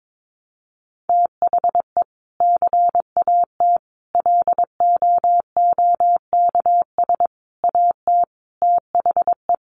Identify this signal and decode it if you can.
Morse code, heard as a keyed tone, about 22 wpm, 700 Hz. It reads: THE CAT LOOKS AT THE
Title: Additional Easy Phrases 22wpm